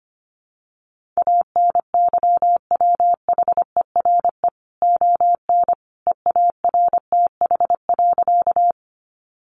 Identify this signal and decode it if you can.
Morse code, heard as a keyed tone, about 25 wpm, 700 Hz. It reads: ANYWHERE ON EARTH.
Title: Sentences from Top 1400 Words 25wpm